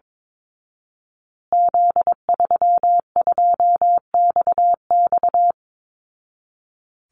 Morse code, heard as a keyed tone, about 22 wpm, 700 Hz. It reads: Z32XX
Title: Call Signs Repeated 2x - 22wpm